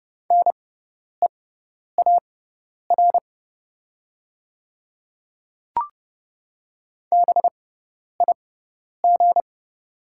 Morse code, heard as a keyed tone, about 30 wpm, spaced as for 10 wpm, 700 Hz. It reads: NEAR BIG
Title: 6x Character Spacing - Top 300 Words - 30wpm